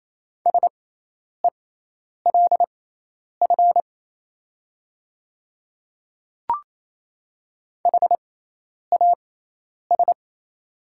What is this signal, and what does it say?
Morse code, heard as a keyed tone, about 28 wpm, spaced as for 10 wpm, 690 Hz. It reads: SELF HAS
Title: 6x Character Spacing - Top 300 Words - 28wpm